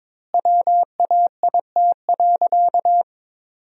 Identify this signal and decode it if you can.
Morse code, heard as a keyed tone, about 21 wpm, 700 Hz. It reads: WAIT.
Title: 3x Word-Spacing - Sentences from Top 500 Words - 22wpm